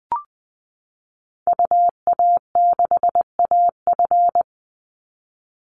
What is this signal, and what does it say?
Morse code, heard as a keyed tone, about 20 wpm, 700 Hz. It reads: UA6AF
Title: Call Signs Repeated 3x - 20wpm